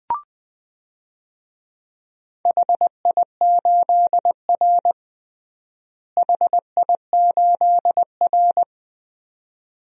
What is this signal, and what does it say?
Morse code, heard as a keyed tone, about 21 wpm, 700 Hz. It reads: HI8R HI8R
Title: DXCC-100 Call Signs - 4 Characters - Repeated 2x - 20wpm